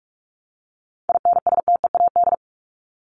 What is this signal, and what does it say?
Morse code, heard as a keyed tone, about 44 wpm, 700 Hz. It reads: INSTEAD
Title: Top 501-750 Words 45wpm